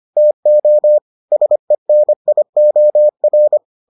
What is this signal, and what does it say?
Morse code, heard as a keyed tone, about 25 wpm, 600 Hz. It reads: TO SENIOR